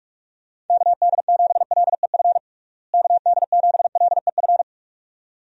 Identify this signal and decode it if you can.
Morse code, heard as a keyed tone, about 46 wpm, 700 Hz. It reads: KD7LEF KD7LEF
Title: USA Call Signs - 6 Characters - Repeated 2x - 45wpm